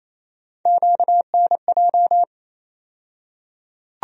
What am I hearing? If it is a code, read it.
Morse code, heard as a keyed tone, about 28 wpm, 700 Hz. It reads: QNJ
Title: USA Call Sign Postfixes - 3 Letters 28wpm